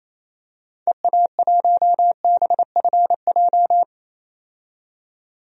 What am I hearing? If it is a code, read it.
Morse code, heard as a keyed tone, about 28 wpm, 700 Hz. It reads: EA1BFJ
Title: DXCC-100 Call Signs - 6 Characters - 28wpm